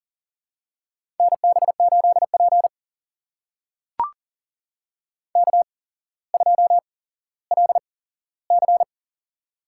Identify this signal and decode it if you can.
Morse code, heard as a keyed tone, about 40 wpm, spaced as for 17 wpm, 700 Hz. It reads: NB8P K2LC
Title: USA Call Signs - 4 Characters - 8x Character Spacing - 40wpm